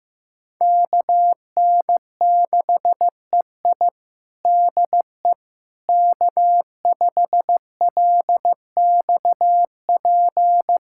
Morse code, heard as a keyed tone, about 16 wpm, 700 Hz. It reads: KN6EI DE K5LXP
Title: USA Callsign DE Callsign - 5 Characters - 15wpm